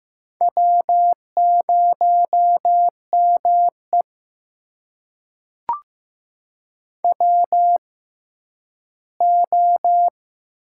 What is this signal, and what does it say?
Morse code, heard as a keyed tone, about 15 wpm, spaced as for 10 wpm, 700 Hz. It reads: W0ME WO
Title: USA Call Signs - 4 Characters - 6x Character Spacing - 15wpm